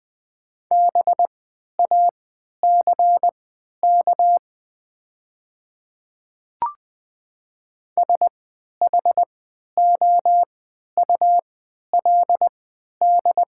Morse code, heard as a keyed tone, about 20 wpm, spaced as for 11 wpm, 700 Hz. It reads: BACK SHOULD
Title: 3x Character Spacing - Top 500 Words - 20wpm